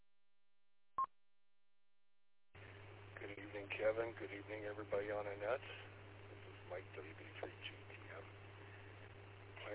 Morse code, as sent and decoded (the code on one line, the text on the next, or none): none